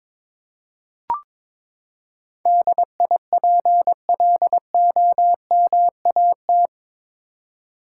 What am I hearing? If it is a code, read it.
Morse code, heard as a keyed tone, about 22 wpm, 700 Hz. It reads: DIPLOMAT
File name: practice / 22wpm / Eight Letter Words - 22wpm